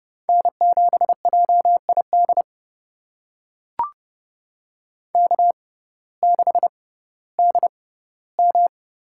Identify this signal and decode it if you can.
Morse code, heard as a keyed tone, about 30 wpm, spaced as for 17 wpm, 700 Hz. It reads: N7JID K6DM
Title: USA Call Signs - 5 Characters - 6x Character Spacing - 30wpm